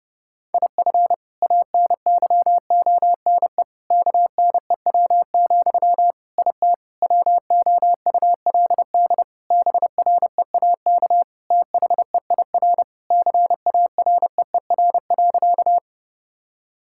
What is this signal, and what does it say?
Morse code, heard as a keyed tone, about 32 wpm, 700 Hz. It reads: IF ANYONE KNEW, IT WOULD BREAK THEIR CAREER.
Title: Sentences from Top 1000 Words - Review Part 6 - 30wpm